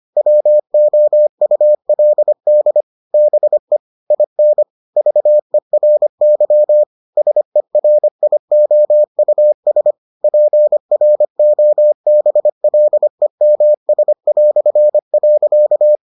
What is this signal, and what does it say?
Morse code, heard as a keyed tone, about 25 wpm, 600 Hz. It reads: WOULD BE IN VERY SERIOUS PROBLEMS".